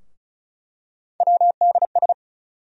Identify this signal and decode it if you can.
Morse code, heard as a keyed tone, about 34 wpm, 700 Hz. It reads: WDS
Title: USA Call Sign Postfixes - 3 Letters 35wpm